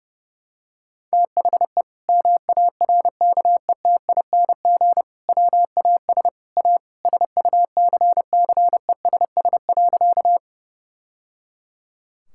Morse code, heard as a keyed tone, about 32 wpm, 700 Hz. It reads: THE MARKETING WAS A SUCCESS.
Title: Sentences from Top 1000 Words - Review Part 5 - 30wpm